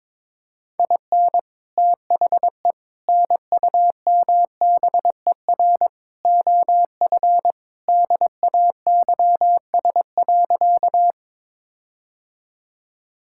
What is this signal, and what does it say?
Morse code, heard as a keyed tone, about 22 wpm, 700 Hz. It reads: IN THE NUMBER OF DAYS.
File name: practice / 22wpm / Sentences with 4-Word Phrases - 22wpm